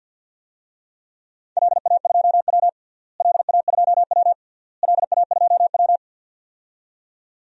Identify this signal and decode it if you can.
Morse code, heard as a keyed tone, about 47 wpm, 700 Hz. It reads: LA2W LA2W LA2W
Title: DXCC-100 Call Signs - 4 Characters - Repeated 3x - 50wpm